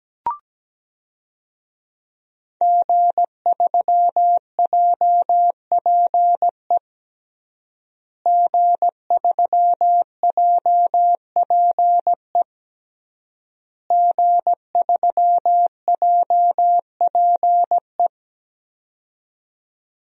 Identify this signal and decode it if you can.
Morse code, heard as a keyed tone, about 17 wpm, 700 Hz. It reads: G3JPE G3JPE G3JPE
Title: DXCC-100 Call Signs - 5 Characters - Repeated 3x - 17wpm